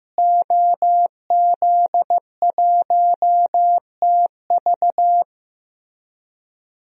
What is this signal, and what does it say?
Morse code, heard as a keyed tone, about 15 wpm, 700 Hz. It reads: OZ1TV
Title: DXCC-100 Call Signs - 5 Characters - Repeated 3x - 15wpm